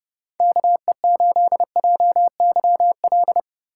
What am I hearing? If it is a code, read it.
Morse code, heard as a keyed tone, about 32 wpm, 700 Hz. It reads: KE8JYL